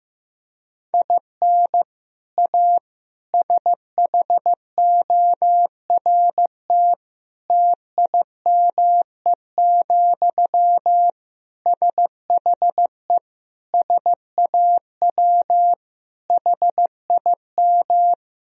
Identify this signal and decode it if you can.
Morse code, heard as a keyed tone, about 15 wpm, 700 Hz. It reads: IN A SHORT TIME, SHE SAW HIM
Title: Sentences from Top 1000 Words - Review Part 2 - 15wpm